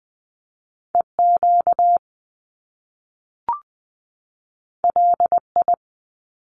Morse code, heard as a keyed tone, about 20 wpm, 700 Hz. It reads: EQ LI
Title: Top Two Letter Combinations in Words 20wpm